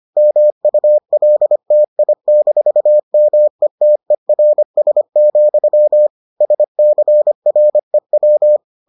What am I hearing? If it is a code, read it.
Morse code, heard as a keyed tone, about 25 wpm, 600 Hz. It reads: MULTI-METERS, SCREW